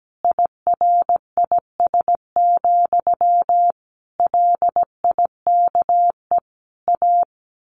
Morse code, heard as a keyed tone, about 17 wpm, 700 Hz. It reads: IRIS, LIKE A
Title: A Sound Of Thunder 17wpm